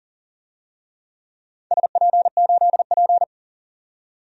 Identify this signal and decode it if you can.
Morse code, heard as a keyed tone, about 40 wpm, 700 Hz. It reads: SP8P
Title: DXCC-100 Call Signs - 4 Characters - 40wpm